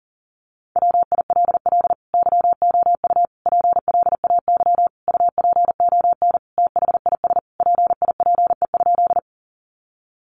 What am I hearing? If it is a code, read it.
Morse code, heard as a keyed tone, about 37 wpm, 700 Hz. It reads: WILL YOU PLAY UPON THIS PIPE?
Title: Sentences from Top 1500 Words 40wpm